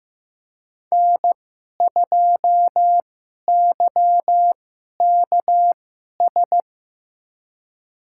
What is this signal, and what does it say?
Morse code, heard as a keyed tone, about 16 wpm, spaced as for 11 wpm, 700 Hz. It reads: N2YKS
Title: USA Call Signs - 5 Characters - 2x Character Spacing - 15wpm